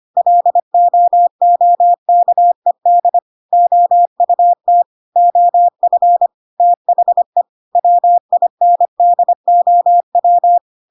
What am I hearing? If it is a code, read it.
Morse code, heard as a keyed tone, about 25 wpm, 700 Hz. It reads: LOOKED OUT OF THE WINDOW
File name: Pinocchio / Pinocchio ch7 25 25 ik1hge